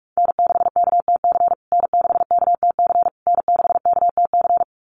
Morse code, heard as a keyed tone, about 44 wpm, 700 Hz. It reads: N6KTC N6KTC N6KTC
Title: USA Call Signs Repeated 3x 45wpm